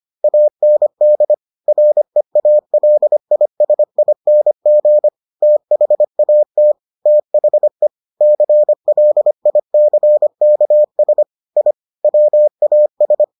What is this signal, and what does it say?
Morse code, heard as a keyed tone, about 25 wpm, 600 Hz. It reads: AND REALISING THAT THE CLICKS I WAS